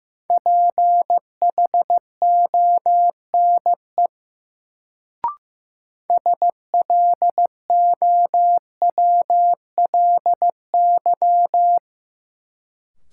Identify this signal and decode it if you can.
Morse code, heard as a keyed tone, about 15 wpm, 700 Hz. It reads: PHONE SLOWLY
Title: Sentences from Top 1100 Words - New Word Drill - 15wpm